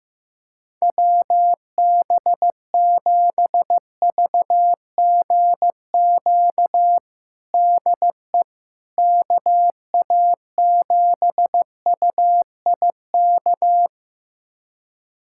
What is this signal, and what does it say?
Morse code, heard as a keyed tone, about 16 wpm, 700 Hz. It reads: WB7VGQ DE KA7UIK
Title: USA Callsign DE Callsign - 6 Characters - 15wpm